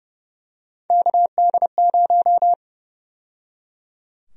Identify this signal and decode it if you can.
Morse code, heard as a keyed tone, about 30 wpm, 700 Hz. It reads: KD0
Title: USA Call Sign Prefixes - 30wpm